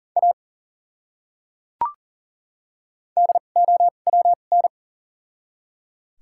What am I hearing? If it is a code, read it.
Morse code, heard as a keyed tone, about 40 wpm, spaced as for 29 wpm, 700 Hz. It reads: A DOWN
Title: 2x Character Spacing - Top 200 Words - 40wpm